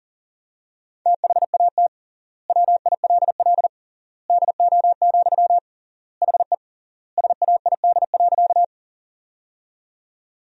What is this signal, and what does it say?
Morse code, heard as a keyed tone, about 40 wpm, 700 Hz. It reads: THAT WILL DO, HE SAID.
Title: 3x Word-Spacing - Sentences from Top 100 Words - 40wpm